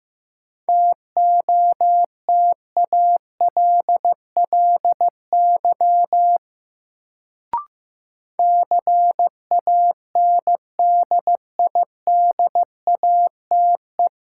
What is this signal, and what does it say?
Morse code, heard as a keyed tone, about 15 wpm, 700 Hz. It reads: TOTALLY CANDIDATE
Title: Sentences from Top 1600 Words - New Word Drill Alt - 15wpm